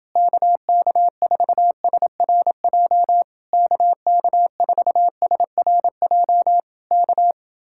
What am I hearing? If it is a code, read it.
Morse code, heard as a keyed tone, about 28 wpm, 700 Hz. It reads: KK4SRJ KK4SRJ K